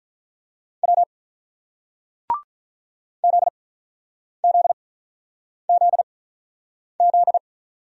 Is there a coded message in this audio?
Yes: Morse code from 0.8 to 7.4 s, heard as a keyed tone, about 47 wpm, spaced as for 9 wpm, 700 Hz. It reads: W ZZZZ